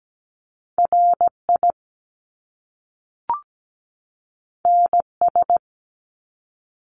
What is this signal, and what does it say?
Morse code, heard as a keyed tone, about 17 wpm, 700 Hz. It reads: RI NS
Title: Top Two Letter Combinations in Words 17wpm